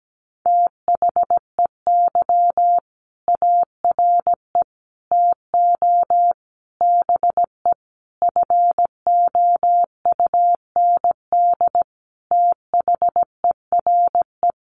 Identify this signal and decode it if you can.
Morse code, heard as a keyed tone, about 17 wpm, 700 Hz. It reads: THEY ARE TO BE FOUND THERE
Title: Sentences from Top 1400 Words 17wpm